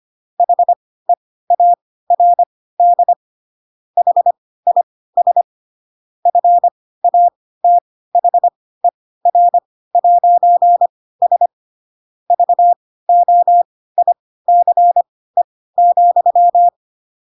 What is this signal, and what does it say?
Morse code, heard as a keyed tone, about 25 wpm, spaced as for 16 wpm, 700 Hz. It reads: HEARD HIS FATHER'S VOICE,